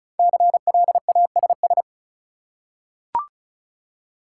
Morse code, heard as a keyed tone, about 36 wpm, 700 Hz. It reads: CLASS
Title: Five Letter Words - 35wpm